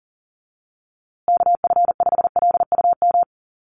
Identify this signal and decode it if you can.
Morse code, heard as a keyed tone, about 40 wpm, 700 Hz. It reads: KF5LUM
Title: USA Call Signs Repeated 2x 40wpm